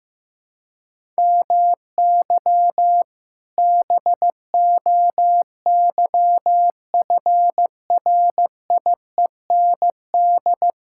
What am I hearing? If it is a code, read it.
Morse code, heard as a keyed tone, about 16 wpm, 700 Hz. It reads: MY BOYFRIEND
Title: Sentences from Top 1000 Words - Review Part 5 - 15wpm